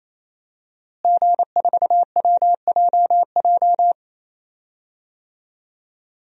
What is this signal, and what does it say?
Morse code, heard as a keyed tone, about 28 wpm, 700 Hz. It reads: G4WJJ
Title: DXCC-100 Call Signs - 5 Characters - Repeated 3x - 28wpm